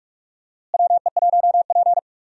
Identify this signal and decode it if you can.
Morse code, heard as a keyed tone, about 46 wpm, 700 Hz. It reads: WE1P
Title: USA Call Signs - 4 Characters - Repeated 3x - 45wpm